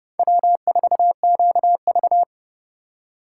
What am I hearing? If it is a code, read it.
Morse code, heard as a keyed tone, about 39 wpm, 700 Hz. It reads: W4QV